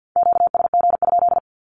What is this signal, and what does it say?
Morse code, heard as a keyed tone, about 50 wpm, 700 Hz. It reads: QSG?
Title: Q-Codes Repeated 3x 50wpm